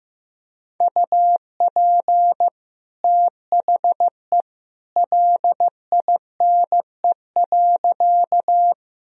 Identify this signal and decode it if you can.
Morse code, heard as a keyed tone, about 15 wpm, 700 Hz. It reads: UP THE LINE.
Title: Sentences from Top 1000 Words - Review Part 1 - 15wpm